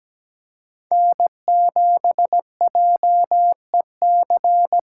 Morse code, heard as a keyed tone, about 17 wpm, 700 Hz. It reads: N7JEC
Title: USA Call Signs - 5 Characters - 3x Character Spacing - 17wpm